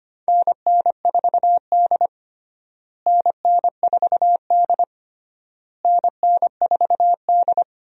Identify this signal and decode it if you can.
Morse code, heard as a keyed tone, about 25 wpm, 700 Hz. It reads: NN4D NN4D NN4D
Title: USA Call Signs - 4 Characters - Repeated 3x - 25wpm